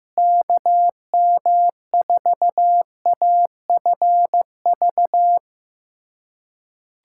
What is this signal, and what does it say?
Morse code, heard as a keyed tone, about 15 wpm, 700 Hz. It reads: KM4AFV
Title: USA Callsign DE Callsign - 6 Characters - 15wpm